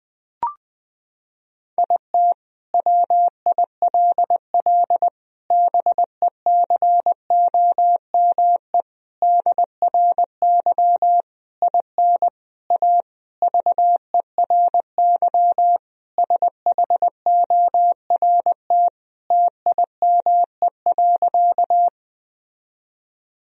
Morse code, heard as a keyed tone, about 21 wpm, 700 Hz. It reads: IT WILL BECOME DRY IN A VERY SHORT TIME.
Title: Sentences from Top 1000 Words - Review Part 3 - 20wpm